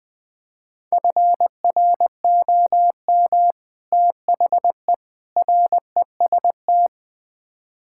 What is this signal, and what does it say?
Morse code, heard as a keyed tone, about 21 wpm, 700 Hz. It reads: FROM THE REST
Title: Build A Phrase - 5-Word Phrases - 20wpm